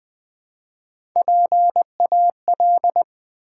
Morse code, heard as a keyed tone, about 20 wpm, 700 Hz. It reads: PAL
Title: Build A Word - 1 Letter At A Time - 3 to 5-Letter Words - 20wpm